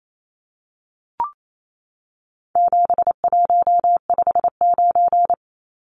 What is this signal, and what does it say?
Morse code, heard as a keyed tone, about 28 wpm, 700 Hz. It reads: 7159
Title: Numbers - 4-Digit 28wpm